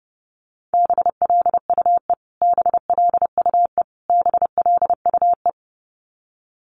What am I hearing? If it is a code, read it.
Morse code, heard as a keyed tone, about 30 wpm, 700 Hz. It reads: BLUE BLUE BLUE
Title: Top 500 Words Repeated 30wpm